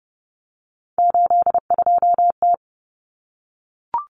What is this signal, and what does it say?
Morse code, heard as a keyed tone, about 30 wpm, 700 Hz. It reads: 82T
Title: Common Cut Numbers - 3-Digit 30wpm